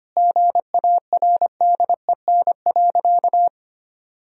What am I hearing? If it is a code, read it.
Morse code, heard as a keyed tone, about 25 wpm, 700 Hz. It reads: GARDEN.
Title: Sentences from Top 1000 Words - Review Part 4 - 25wpm